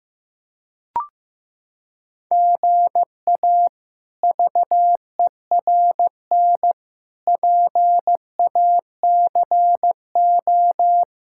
Morse code, heard as a keyed tone, about 15 wpm, 700 Hz. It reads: GA VERN PACO
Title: K1USN SST QSO Practice - 15wpm